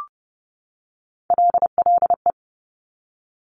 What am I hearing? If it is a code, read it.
Morse code, heard as a keyed tone, about 30 wpm, 700 Hz. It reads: LLE
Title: Top Three Letter Combinations in Words 30wpm